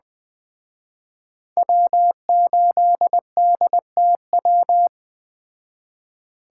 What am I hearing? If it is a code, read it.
Morse code, heard as a keyed tone, about 20 wpm, 700 Hz. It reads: W8DTW
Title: Speed Racing - USA 1x3 Call Signs 17wpm